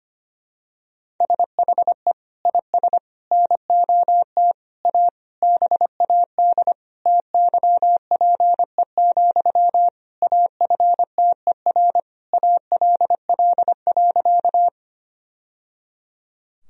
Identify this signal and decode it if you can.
Morse code, heard as a keyed tone, about 25 wpm, 700 Hz. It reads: SHE IS NOT A BAD TYPE, AFTER ALL.
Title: Sentences from Top 1000 Words - Review Part 3 - 25wpm